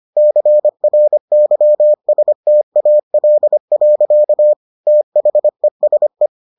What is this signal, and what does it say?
Morse code, heard as a keyed tone, about 25 wpm, 600 Hz. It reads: CRYSTAL. THESE